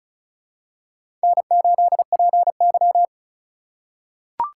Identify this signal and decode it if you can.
Morse code, heard as a keyed tone, about 36 wpm, 700 Hz. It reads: N8PY
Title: USA Call Signs - 4 Characters - 2x Character Spacing - 35wpm